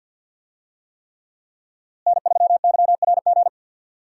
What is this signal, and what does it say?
Morse code, heard as a keyed tone, about 52 wpm, 700 Hz. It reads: N3YRG
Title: USA Call Signs - 5 Characters - Repeated 3x - 50wpm